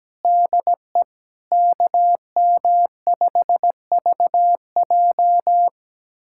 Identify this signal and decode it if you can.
Morse code, heard as a keyed tone, about 17 wpm, 700 Hz. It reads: DE KM5VJ
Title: USA Callsign DE Callsign - 5 Characters - 17wpm